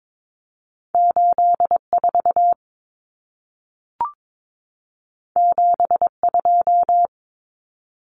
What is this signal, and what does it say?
Morse code, heard as a keyed tone, about 22 wpm, 700 Hz. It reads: 84 72
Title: Common Cut Numbers - 2-Digit 22wpm